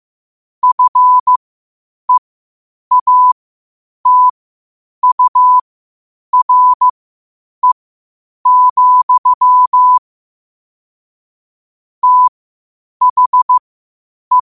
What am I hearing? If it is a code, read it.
Morse code, heard as a keyed tone, about 15 wpm, spaced as for 8 wpm, 1000 Hz. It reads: FEATURE, THE
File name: qotdfiles / 8wpm